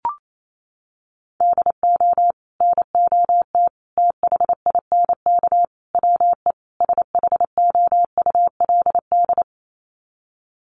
Morse code, heard as a keyed tone, about 28 wpm, 700 Hz. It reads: DO NOT THINK WE SHOULD
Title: Sets of 5 Words Encore 28wpm